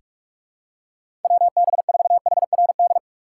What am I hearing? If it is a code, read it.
Morse code, heard as a keyed tone, about 43 wpm, 700 Hz. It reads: WB4HRD
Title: USA Call Signs - 6 Characters - 2x Character Spacing - 45wpm